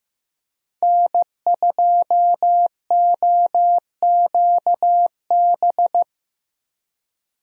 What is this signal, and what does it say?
Morse code, heard as a keyed tone, about 15 wpm, 700 Hz. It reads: N2OQB
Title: USA Call Signs - 5 Characters - Repeated 3x - 15wpm